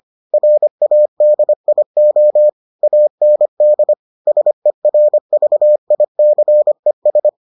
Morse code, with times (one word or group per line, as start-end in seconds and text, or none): 0.3-2.5 RADIO
2.8-3.9 AND
4.3-7.3 SERVICES